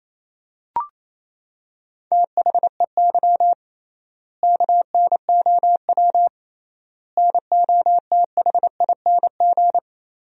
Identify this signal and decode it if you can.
Morse code, heard as a keyed tone, about 28 wpm, 700 Hz. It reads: THEY KNOW NOTHING
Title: 3x Word-Spacing - Sentences from Top 500 Words - 28wpm